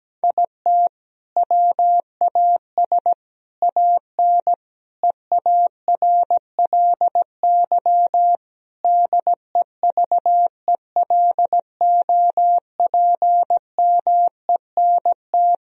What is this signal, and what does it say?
Morse code, heard as a keyed tone, about 17 wpm, 700 Hz. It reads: IT WAS AN EARLY DEVELOPMENT
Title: Sentences from Top 1000 Words - Review Part 4 - 17wpm